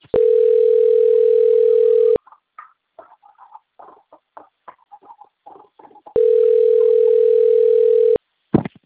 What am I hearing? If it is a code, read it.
Ringback tone (North America), 440 + 480 Hz.
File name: out- - - - - .19204